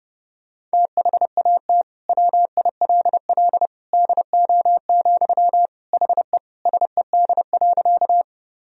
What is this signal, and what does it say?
Morse code, heard as a keyed tone, about 30 wpm, 700 Hz. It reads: THAT WILL DO, HE SED.
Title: Sentences from Top 100 Words and Abbreviations 30wpm